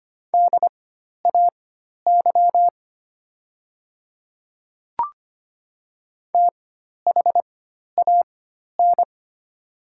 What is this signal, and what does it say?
Morse code, heard as a keyed tone, about 25 wpm, spaced as for 12 wpm, 700 Hz. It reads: DAY THAN
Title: 4x Character Spacing - Top 100 Words - 25wpm